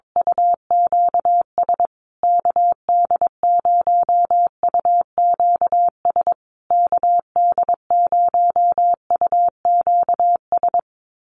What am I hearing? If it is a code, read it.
Morse code, heard as a keyed tone, about 22 wpm, 700 Hz. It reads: UQS KD0UQS KD0UQS
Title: USA Call Signs Repeated 3x 22wpm